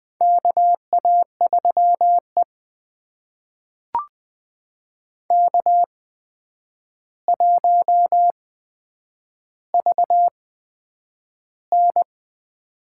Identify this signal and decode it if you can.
Morse code, heard as a keyed tone, about 20 wpm, spaced as for 9 wpm, 700 Hz. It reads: KA3E K1VN
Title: USA Call Signs - 4 Characters - 8x Character Spacing - 20wpm